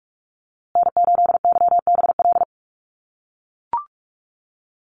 Morse code, heard as a keyed tone, about 44 wpm, 700 Hz. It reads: N8YBL
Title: USA Call Signs Repeated 3x 45wpm